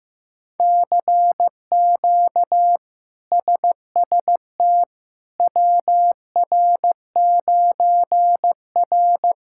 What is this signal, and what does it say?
Morse code, heard as a keyed tone, about 16 wpm, 700 Hz. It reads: CQ SST WR9R
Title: K1USN SST QSO Practice - 15wpm